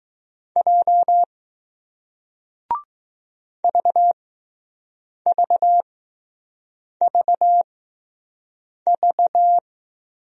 Morse code, heard as a keyed tone, about 20 wpm, spaced as for 6 wpm, 700 Hz. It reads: J VVVV